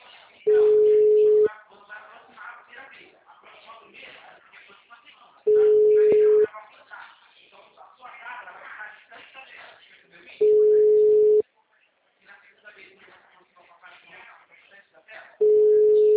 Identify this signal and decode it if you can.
Ringback tone (Europe, 425 Hz), with speech.